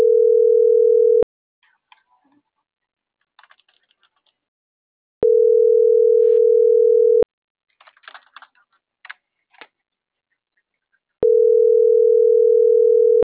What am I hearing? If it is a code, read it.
Ringback tone (North America), 440 + 480 Hz.
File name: CENTRAL MABE / out maria